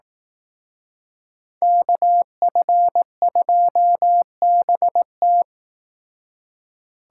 Morse code, heard as a keyed tone, about 18 wpm, 700 Hz. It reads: KF2BT
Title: Speed Racing - USA 2x2 Call Signs 15wpm